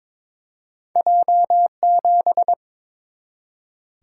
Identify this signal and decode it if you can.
Morse code, heard as a keyed tone, about 22 wpm, 690 Hz. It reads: J7
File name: practice / 22wpm / Two Letter-Number Combinations - 22wpm